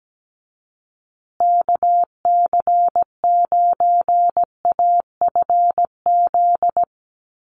Morse code, heard as a keyed tone, about 17 wpm, 700 Hz. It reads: KC9AFZ
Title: USA Call Signs Repeated 3x 17wpm